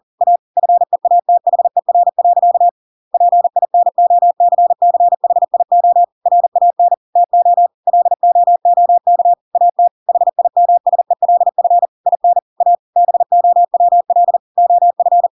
Morse code, heard as a keyed tone, about 40 wpm, 700 Hz. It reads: A FEATHER. PINOCCHIO RAN TO LOOK AT HIMSELF IN A BOWL OF
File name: Pinocchio / Pinocchio ch8 40 40 ik1hge